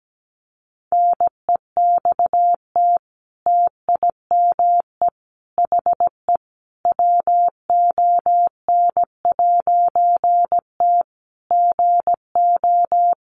Morse code, heard as a keyed tone, about 17 wpm, 700 Hz. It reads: NEXT TIME HE WON'T GO
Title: A Sound Of Thunder 17wpm